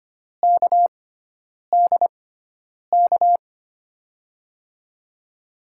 Morse code, heard as a keyed tone, about 25 wpm, spaced as for 9 wpm, 700 Hz. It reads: KDK
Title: USA Call Signs - 5 Characters - 6x Character Spacing - 25wpm